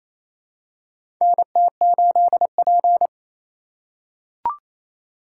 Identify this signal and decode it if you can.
Morse code, heard as a keyed tone, about 28 wpm, 700 Hz. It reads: NT8P
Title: USA Call Signs - 4 Characters - 2x Character Spacing - 28wpm